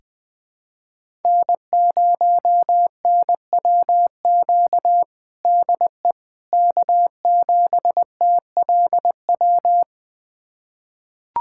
Morse code, heard as a keyed tone, about 20 wpm, 700 Hz. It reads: N0NWQ DE K7TLW
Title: USA Callsign DE Callsign - 5 Characters - 20wpm